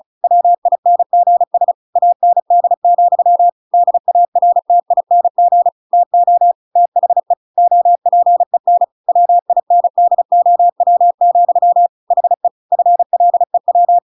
Morse code, heard as a keyed tone, about 34 wpm, 700 Hz. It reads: WINGS AND, DARTING TO THE OPEN WINDOW, HE FLEW